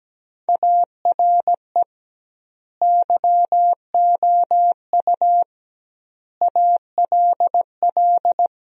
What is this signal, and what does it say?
Morse code, heard as a keyed tone, about 17 wpm, 700 Hz. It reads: ARE YOU ALL
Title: 2x Word-Spacing - Sentences from Top 400 Words - 17wpm